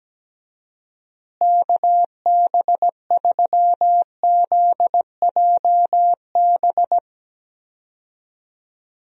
Morse code, heard as a keyed tone, about 17 wpm, 700 Hz. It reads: KB3ZJB
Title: USA Call Signs - 6 Characters - Repeated 3x - 17wpm